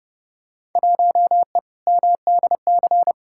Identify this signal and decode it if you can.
Morse code, heard as a keyed tone, about 30 wpm, 700 Hz. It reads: 1E MDC
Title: Field Day Exchanges Encore - 30wpm